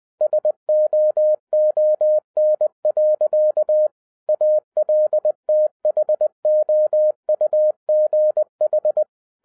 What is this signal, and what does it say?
Morse code, heard as a keyed tone, about 20 wpm, 610 Hz. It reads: SOON. ALTHOUGH
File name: 2012-01-mp3 / n16-THROUGH EIGHT MONTHS-20wpm